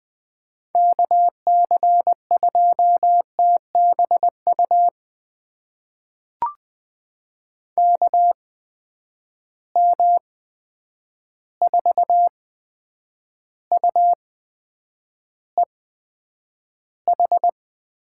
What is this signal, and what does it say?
Morse code, heard as a keyed tone, about 20 wpm, spaced as for 9 wpm, 700 Hz. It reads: KC2TBU KM4UEH
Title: USA Call Signs - 6 Characters - 8x Character Spacing - 20wpm